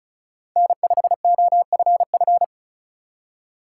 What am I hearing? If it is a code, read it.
Morse code, heard as a keyed tone, about 36 wpm, 700 Hz. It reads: N5OFF
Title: USA Call Signs - 2x Character Spacing - 35wpm